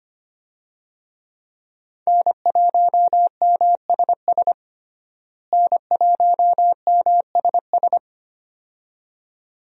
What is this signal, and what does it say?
Morse code, heard as a keyed tone, about 25 wpm, 700 Hz. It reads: N1MSS N1MSS
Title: USA Call Signs - 5 Characters - Repeated 2x - 25wpm